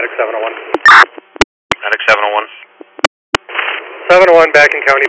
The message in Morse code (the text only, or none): none